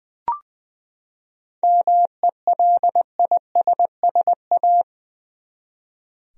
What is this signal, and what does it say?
Morse code, heard as a keyed tone, about 20 wpm, 700 Hz. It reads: MELISSA
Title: USA Common Names - 20wpm